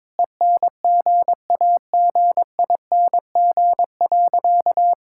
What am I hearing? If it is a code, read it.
Morse code, heard as a keyed tone, about 22 wpm, 700 Hz. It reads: ENGAGING.
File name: practice / 22wpm / Sentences with 2-Word Phrases - 22wpm